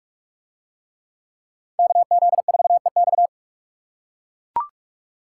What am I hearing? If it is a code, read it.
Morse code, heard as a keyed tone, about 43 wpm, 700 Hz. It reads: KZ4EX